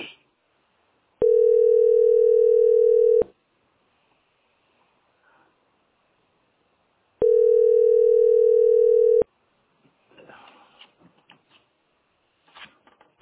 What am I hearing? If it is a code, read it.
Ringback tone (North America), 440 + 480 Hz.